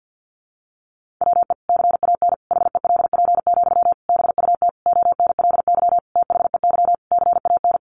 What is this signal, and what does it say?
Morse code, heard as a keyed tone, about 50 wpm, 700 Hz. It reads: WE CAN HELP, BUT ONLY THEY CAN
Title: Sentences from Top 1500 Words 50wpm